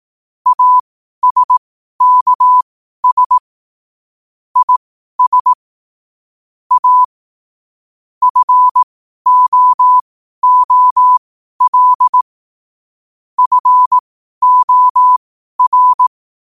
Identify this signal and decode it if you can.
Morse code, heard as a keyed tone, about 18 wpm, spaced as for 12 wpm, 1000 Hz. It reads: ASKS IS A FOOL FOR
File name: qotdfiles / 12wpm